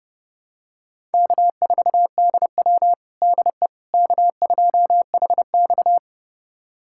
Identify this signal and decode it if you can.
Morse code, heard as a keyed tone, about 30 wpm, 700 Hz. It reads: K4DW DE K2HX
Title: USA Callsign DE Callsign - 4 Characters - 30wpm